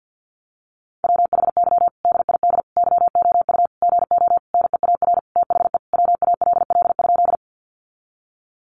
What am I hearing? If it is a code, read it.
Morse code, heard as a keyed tone, about 50 wpm, 700 Hz. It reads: WHY DID YOU GO NEAR THE WALL?
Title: Sentences from Top 1100 Words 50wpm